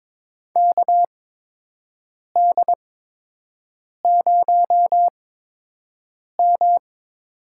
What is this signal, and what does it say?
Morse code, heard as a keyed tone, about 22 wpm, spaced as for 6 wpm, 700 Hz. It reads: KD0M
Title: USA Call Signs - 8x Character Spacing - 22wpm